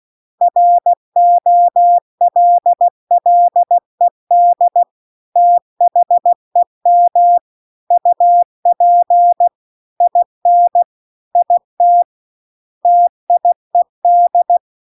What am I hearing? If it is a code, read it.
Morse code, heard as a keyed tone, about 16 wpm, 700 Hz. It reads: ROLLED THEM UP IN IT TIED
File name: CW-The-Big-Twoheaded-River-2-16WPM